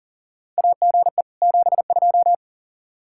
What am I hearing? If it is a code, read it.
Morse code, heard as a keyed tone, about 41 wpm, 700 Hz. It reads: AGE 72
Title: K1USN SST QSO Practice 40wpm